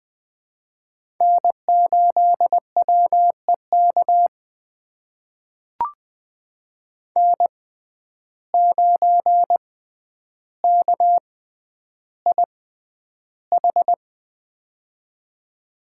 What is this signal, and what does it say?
Morse code, heard as a keyed tone, about 20 wpm, spaced as for 10 wpm, 700 Hz. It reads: N8WEK N9KIH
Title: USA Call Signs - 5 Characters - 6x Character Spacing - 20wpm